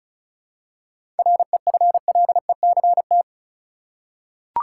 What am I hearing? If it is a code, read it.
Morse code, heard as a keyed tone, about 36 wpm, 700 Hz. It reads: REFLECT
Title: Seven Letter Words - 35wpm